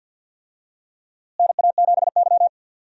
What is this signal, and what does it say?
Morse code, heard as a keyed tone, about 51 wpm, 700 Hz. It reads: NA7Y
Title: USA Call Signs - 4 Characters - Repeated 3x - 50wpm